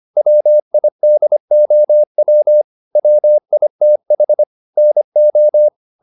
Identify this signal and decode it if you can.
Morse code, heard as a keyed tone, about 25 wpm, 600 Hz. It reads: WIDOW WITH NO